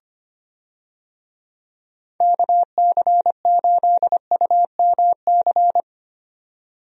Morse code, heard as a keyed tone, about 25 wpm, 700 Hz. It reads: KC8UMC
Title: USA Call Signs - 6 Characters - 25wpm